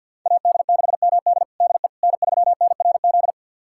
Morse code, heard as a keyed tone, about 53 wpm, 700 Hz. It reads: AD6MD DE N3NRZ
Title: USA Callsign DE Callsign - 5 Characters - 50wpm